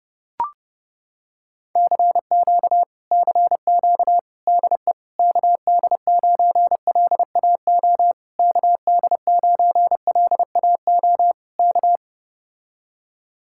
Morse code, heard as a keyed tone, about 30 wpm, 700 Hz. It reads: CQ CQ DE KD9LAO KD9LAO K